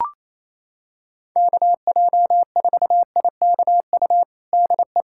Morse code, heard as a keyed tone, about 28 wpm, 700 Hz. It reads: KJ4IKU DE
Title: USA Callsign DE Callsign - 6 Characters - 28wpm